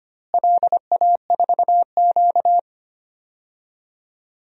Morse code, heard as a keyed tone, about 25 wpm, 700 Hz. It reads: LA4Q
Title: DXCC-100 Call Signs - 4 Characters - Repeated 2x - 25wpm